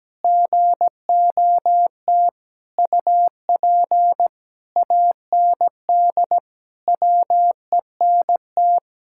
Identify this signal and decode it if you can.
Morse code, heard as a keyed tone, about 17 wpm, 700 Hz. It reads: GOT UP AND WENT
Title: Sentences from Top 1600 Words 17wpm